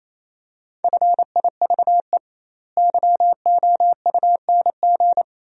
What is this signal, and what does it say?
Morse code, heard as a keyed tone, about 28 wpm, 700 Hz. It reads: FIVE YOUNG
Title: 2x Word-Spacing - Sentences from Top 400 Words - 28wpm